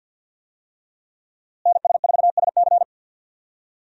Morse code, heard as a keyed tone, about 51 wpm, 700 Hz. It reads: NS4SC
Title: USA Call Signs - 5 Characters - Repeated 2x - 50wpm